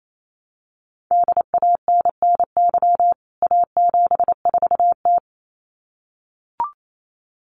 Morse code, heard as a keyed tone, about 28 wpm, 700 Hz. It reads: DANNY A74T
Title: CWT Exchange 28wpm